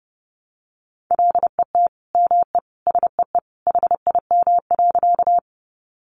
Morse code, heard as a keyed tone, about 30 wpm, 700 Hz. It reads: LET ME SEE HIM.
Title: Sentences from Top 300 Words 30wpm